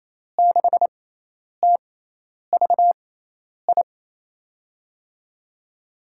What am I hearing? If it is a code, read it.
Morse code, heard as a keyed tone, about 28 wpm, spaced as for 10 wpm, 700 Hz. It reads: 6TVI
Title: USA Call Signs - 6x Character Spacing - 28wpm